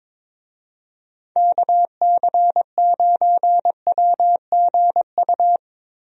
Morse code, heard as a keyed tone, about 22 wpm, 700 Hz. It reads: KC9WGU
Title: USA Call Signs - 6 Characters - Repeated 3x - 22wpm